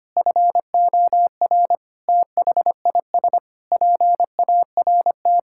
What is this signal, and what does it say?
Morse code, heard as a keyed tone, about 25 wpm, 700 Hz. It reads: FOR THIS PART
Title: Sentences from Top 1000 Words - Review Part 5 - 25wpm